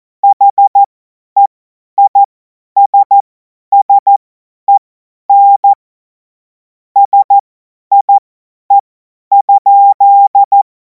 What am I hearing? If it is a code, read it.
Morse code, heard as a keyed tone, about 14 wpm, spaced as for 10 wpm, 800 Hz. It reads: HEISSEN SIE?
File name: Morsekurs / Witze / Witz 12. Mai 2026 50 F70